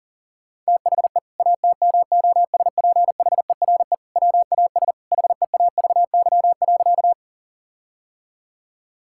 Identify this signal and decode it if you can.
Morse code, heard as a keyed tone, about 40 wpm, 700 Hz. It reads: THE ATMOSPHERE WAS HEAVY.
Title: Sentences from Top 1600 Words 40wpm